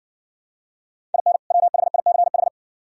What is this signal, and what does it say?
Morse code, heard as a keyed tone, about 60 wpm, 700 Hz. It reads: IN WHICH